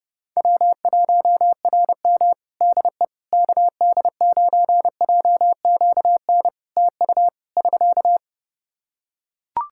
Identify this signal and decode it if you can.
Morse code, heard as a keyed tone, about 31 wpm, 700 Hz. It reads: W1RM DE KD9JQN TU <SK>
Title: K1USN SST QSO Practice 30wpm